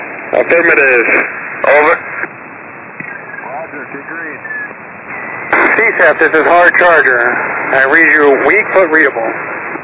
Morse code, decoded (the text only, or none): none